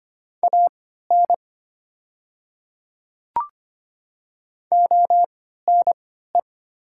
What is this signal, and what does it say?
Morse code, heard as a keyed tone, about 25 wpm, spaced as for 14 wpm, 700 Hz. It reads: AN ONE
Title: 3x Character Spacing - Top 100 Words - 25wpm